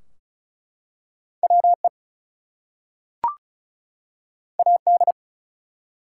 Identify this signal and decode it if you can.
Morse code, heard as a keyed tone, about 36 wpm, 700 Hz. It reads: WE AD
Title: Two Letter Words - 35wpm